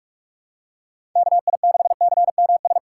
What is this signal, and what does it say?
Morse code, heard as a keyed tone, about 46 wpm, 700 Hz. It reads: KI6CMS
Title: USA Call Signs - 6 Characters - 3x Character Spacing - 45wpm